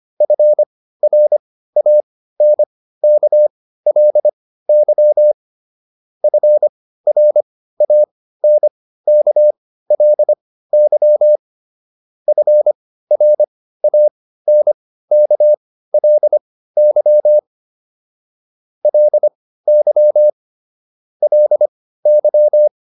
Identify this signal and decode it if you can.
Morse code, heard as a keyed tone, about 25 wpm, spaced as for 15 wpm, 600 Hz. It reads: FRANKLY FRANKLY FRANKLY LY LY